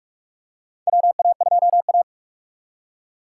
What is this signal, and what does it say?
Morse code, heard as a keyed tone, about 47 wpm, 700 Hz. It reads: WA2A
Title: USA Call Signs - 4 Characters - 2x Character Spacing - 45wpm